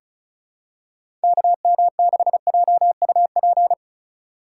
Morse code, heard as a keyed tone, about 36 wpm, 700 Hz. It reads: KM6JUP
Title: USA Call Signs - 6 Characters - 3x Character Spacing - 35wpm